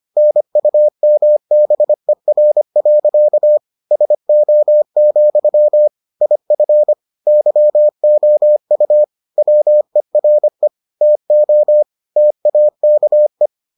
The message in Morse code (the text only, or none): NUMBER. SO, IF YOU WERE TO TAKE